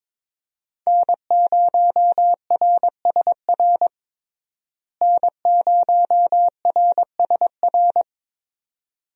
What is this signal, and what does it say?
Morse code, heard as a keyed tone, about 22 wpm, 700 Hz. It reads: N0RSR N0RSR
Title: USA Call Signs - 5 Characters - Repeated 2x - 22wpm